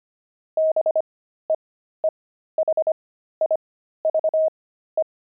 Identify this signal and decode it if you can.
Morse code, heard as a keyed tone, about 25 wpm, spaced as for 13 wpm, 630 Hz. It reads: BEEHIVE
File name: pangram-fs-13-cs-25-beehive